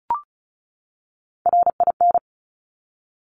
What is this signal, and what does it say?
Morse code, heard as a keyed tone, about 36 wpm, 700 Hz. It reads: RIN
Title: Top Three Letter Combinations in Words 35wpm